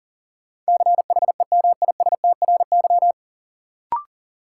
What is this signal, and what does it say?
Morse code, heard as a keyed tone, about 38 wpm, 700 Hz. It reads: CHEMISTRY
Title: Sentences from Top 1000 Words - Review Part 6 - New Word Drill - 40wpm